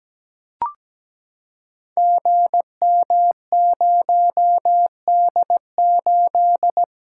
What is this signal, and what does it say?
Morse code, heard as a keyed tone, about 17 wpm, 700 Hz. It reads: GM0D8
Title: Code Groups - 5 Letter:Numbers - 17wpm